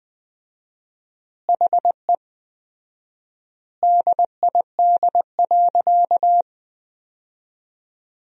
Morse code, heard as a keyed tone, about 20 wpm, 700 Hz. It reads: HE DID.
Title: 4x Word-Spacing - Sentences from Top 500 Words - 20wpm